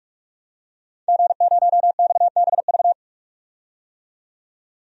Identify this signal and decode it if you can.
Morse code, heard as a keyed tone, about 42 wpm, 700 Hz. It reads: G0XBV
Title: DXCC-100 Call Signs - 5 Characters - 45wpm